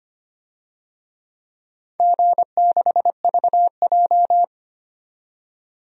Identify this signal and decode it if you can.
Morse code, heard as a keyed tone, about 25 wpm, 700 Hz. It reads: G6VJ
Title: DXCC-100 Call Signs - 4 Characters - 25wpm